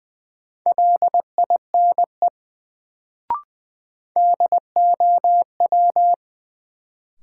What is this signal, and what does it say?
Morse code, heard as a keyed tone, about 20 wpm, 700 Hz. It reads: LINE DOW
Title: Sets of 5 Words - New Word Drill - 20wpm